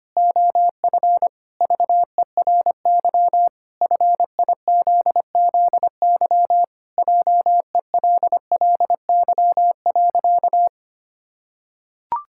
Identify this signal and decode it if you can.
Morse code, heard as a keyed tone, about 25 wpm, 700 Hz. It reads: OF VERY FIZZY JELLY.
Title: Pangram Challenge v2 - 25wpm